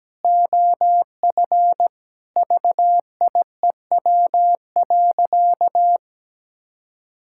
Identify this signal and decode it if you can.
Morse code, heard as a keyed tone, about 17 wpm, 700 Hz. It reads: OF VIEW.
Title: Sentences from Top 1000 Words - Review Part 6 - 17wpm